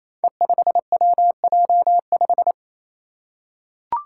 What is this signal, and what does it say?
Morse code, heard as a keyed tone, about 28 wpm, 700 Hz. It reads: E5WJ5
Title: Five Challenging Letter-Number Combinations 28wpm